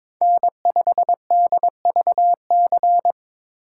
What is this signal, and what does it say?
Morse code, heard as a keyed tone, about 22 wpm, 700 Hz. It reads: N5DVC
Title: USA Call Signs - 5 Characters - 3x Character Spacing - 22wpm